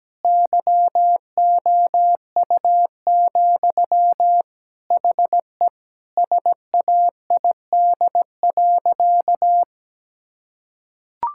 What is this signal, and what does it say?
Morse code, heard as a keyed tone, about 17 wpm, 700 Hz. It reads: YOU, HE SAID.
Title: Sentences from Top 1000 Words - Review Part 6 - 17wpm